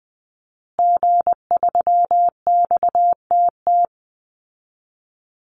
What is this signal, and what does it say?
Morse code, heard as a keyed tone, about 20 wpm, 700 Hz. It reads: G3XTT
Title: Call Signs Repeated 2x - 20wpm